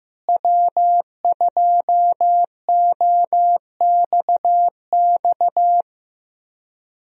Morse code, heard as a keyed tone, about 15 wpm, 700 Hz. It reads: W2OXX
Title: USA Call Signs - 5 Characters - Repeated 2x - 15wpm